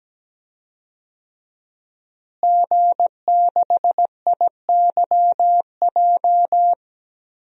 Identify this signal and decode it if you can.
Morse code, heard as a keyed tone, about 17 wpm, 700 Hz. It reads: G6IYJ
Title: DXCC-100 Call Signs - 5 Characters - Repeated 3x - 17wpm